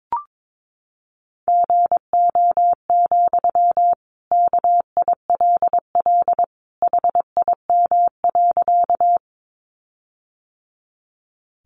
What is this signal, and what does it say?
Morse code, heard as a keyed tone, about 22 wpm, 700 Hz. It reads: GO, KILL HIM.
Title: Sentences from Top 1400 Words 22wpm